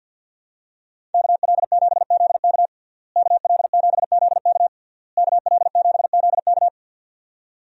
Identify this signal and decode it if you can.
Morse code, heard as a keyed tone, about 50 wpm, 700 Hz. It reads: KL7ZK KL7ZK KL7ZK
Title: USA Call Signs - 5 Characters - Repeated 3x - 50wpm